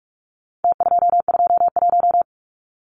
Morse code, heard as a keyed tone, about 44 wpm, 700 Hz. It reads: T221
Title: Numbers - Contest Style - 4-Digit 45wpm